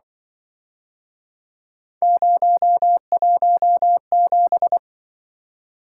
Morse code, heard as a keyed tone, about 25 wpm, 700 Hz. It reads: 017
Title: Speed-Racing - Three Letter-Number Combinations - 20wpm